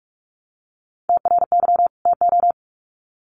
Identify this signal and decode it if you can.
Morse code, heard as a keyed tone, about 44 wpm, 700 Hz. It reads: TRY TO
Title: Sets of 2 Words 45wpm